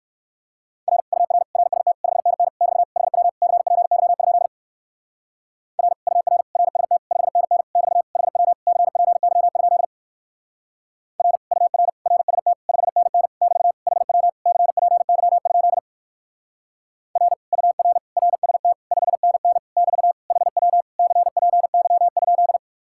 Morse code, heard as a keyed tone, about 57 wpm, 700 Hz. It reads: R UR RST 5NN <BT> HW CPY? R UR RST 5NN <BT> HW CPY? R UR RST 5NN <BT> HW CPY? R UR RST 5NN <BT> HW CPY?